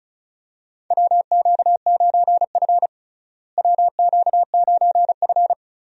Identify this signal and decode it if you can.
Morse code, heard as a keyed tone, about 36 wpm, 700 Hz. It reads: WQ9F WQ9F
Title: USA Call Signs - 4 Characters - Repeated 2x - 35wpm